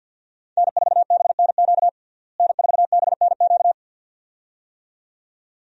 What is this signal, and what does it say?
Morse code, heard as a keyed tone, about 50 wpm, 700 Hz. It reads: N4BNQ N4BNQ
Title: USA Call Signs - 5 Characters - Repeated 2x - 50wpm